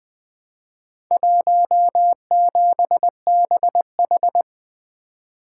Morse code, heard as a keyed tone, about 20 wpm, 700 Hz. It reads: 17BH
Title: Four Letter-Number Combinations - 20wpm